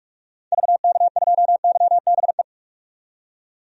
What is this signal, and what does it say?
Morse code, heard as a keyed tone, about 44 wpm, 690 Hz. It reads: VK2YBE
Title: DXCC-100 Call Signs - 6 Characters - 45wpm